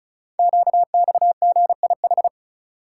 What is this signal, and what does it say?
Morse code, heard as a keyed tone, about 36 wpm, 690 Hz. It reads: QXGIH
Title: Code Groups - 5 Letters - 35wpm